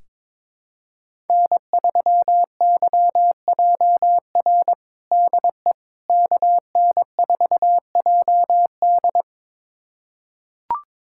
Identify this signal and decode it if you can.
Morse code, heard as a keyed tone, about 21 wpm, 700 Hz. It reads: N3YJR DE KN4JD
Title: USA Callsign DE Callsign - 5 Characters - 22wpm